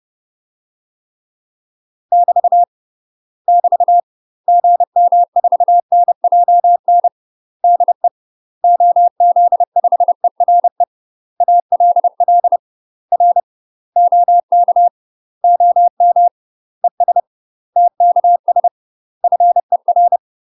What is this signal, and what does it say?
Morse code, heard as a keyed tone, about 30 wpm, 700 Hz. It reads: <BT> <BT> GM4NJN DE OZ5ERE ALL R OK OM ES TKS FER
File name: UK MASTER QSOs TRAINING / 30WPM / UK MASTER QSOs 0007